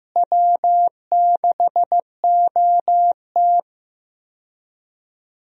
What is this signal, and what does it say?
Morse code, heard as a keyed tone, about 15 wpm, 700 Hz. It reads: W6OT
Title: USA Call Signs - 4 Characters - Repeated 3x - 15wpm